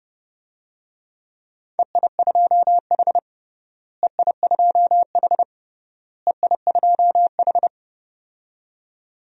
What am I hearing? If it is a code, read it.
Morse code, heard as a keyed tone, about 30 wpm, 700 Hz. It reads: EI2H EI2H EI2H
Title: DXCC-100 Call Signs - 4 Characters - Repeated 3x - 30wpm